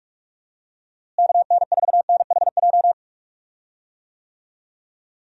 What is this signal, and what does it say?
Morse code, heard as a keyed tone, about 46 wpm, 700 Hz. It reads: KN4NHJ
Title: USA Call Signs - 6 Characters - Repeated 2x - 45wpm